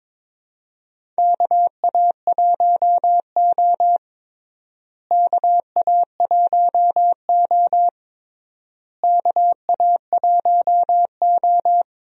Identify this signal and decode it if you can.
Morse code, heard as a keyed tone, about 22 wpm, 700 Hz. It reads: KA1O KA1O KA1O
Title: USA Call Signs - 4 Characters - Repeated 3x - 22wpm